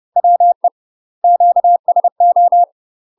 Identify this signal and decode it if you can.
Morse code, heard as a keyed tone, about 30 wpm, 700 Hz. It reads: WE QSO